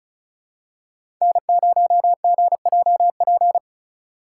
Morse code, heard as a keyed tone, about 34 wpm, 700 Hz. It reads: N0GJP
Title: USA Call Signs - 5 Characters - 6x Character Spacing - 35wpm